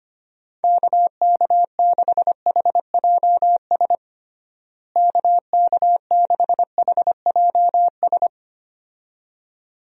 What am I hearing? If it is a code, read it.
Morse code, heard as a keyed tone, about 25 wpm, 700 Hz. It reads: KK6HJS KK6HJS
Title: USA Call Signs - 6 Characters - Repeated 2x - 25wpm